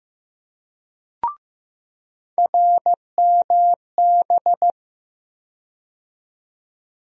Morse code, heard as a keyed tone, about 15 wpm, 700 Hz. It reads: RMB
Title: USA Call Sign Postfixes - 3 Letters 15wpm